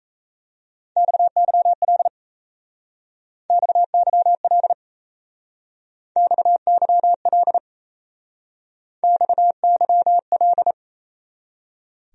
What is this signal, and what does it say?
Morse code, heard as a keyed tone, about 37 wpm, 700 Hz. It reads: XYL XYL XYL XYL